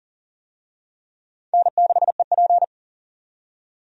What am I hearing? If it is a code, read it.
Morse code, heard as a keyed tone, about 40 wpm, 700 Hz. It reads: N6EP
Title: USA Call Signs - 4 Characters - 40wpm